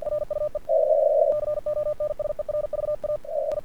Morse code, heard as a keyed tone, about 48 wpm, 1200 Hz. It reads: RUE CONSERVA I
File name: archived / CRSnet